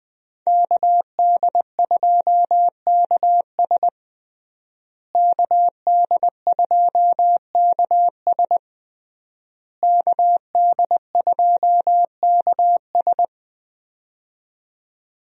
Morse code, heard as a keyed tone, about 20 wpm, 700 Hz. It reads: KD2KS KD2KS KD2KS
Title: USA Call Signs - 5 Characters - Repeated 3x - 20wpm